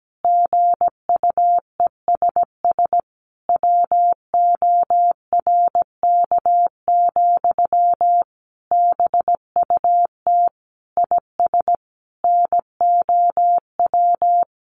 Morse code, heard as a keyed tone, about 17 wpm, 700 Hz. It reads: GUESS WORK, BUT IS NOW
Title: Sentences from Top 1500 Words 17wpm